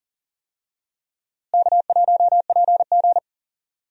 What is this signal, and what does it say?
Morse code, heard as a keyed tone, about 40 wpm, 700 Hz. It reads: K1PG
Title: USA Call Signs - 4 Characters - Repeated 3x - 40wpm